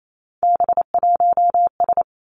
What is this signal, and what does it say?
Morse code, heard as a keyed tone, about 28 wpm, 700 Hz. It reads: B1S
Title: Three Challenging Letter-Number Combinations 28wpm